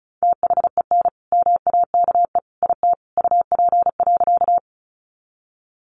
Morse code, heard as a keyed tone, about 34 wpm, 700 Hz. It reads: THEN MAKE IT UP.
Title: Sentences from Top 100 Words 35wpm